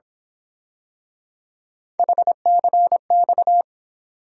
Morse code, heard as a keyed tone, about 26 wpm, 700 Hz. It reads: HCX
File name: Speed-Racing - Three Letter-Number Combinations - 22wpm